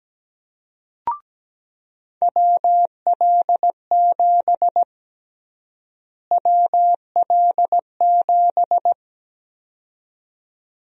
Morse code, heard as a keyed tone, about 17 wpm, 700 Hz. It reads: WL7 WL7
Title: USA Call Sign Prefixes - Repeated 2x - 17wpm